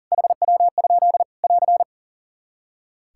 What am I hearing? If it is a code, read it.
Morse code, heard as a keyed tone, about 40 wpm, 700 Hz. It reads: HW? <AR>
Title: Practice QSO - Exchange 1 - Beginning - 40wpm